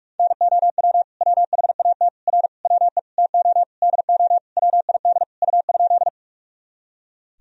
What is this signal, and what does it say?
Morse code, heard as a keyed tone, about 46 wpm, 700 Hz. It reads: NOW WHAT R WE TO DO WID U?
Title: Sentences from Top 100 Words and Abbreviations 45wpm